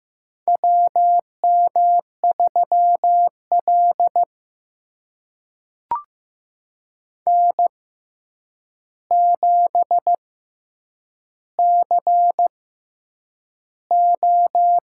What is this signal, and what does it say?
Morse code, heard as a keyed tone, about 15 wpm, spaced as for 8 wpm, 700 Hz. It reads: WM3L N7CO
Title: USA Call Signs - 4 Characters - 6x Character Spacing - 15wpm